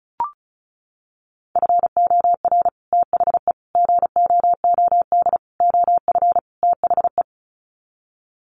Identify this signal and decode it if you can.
Morse code, heard as a keyed tone, about 34 wpm, 700 Hz. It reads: FOR THE GOOD OF THE
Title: Sets of 5 Words Encore2 35wpm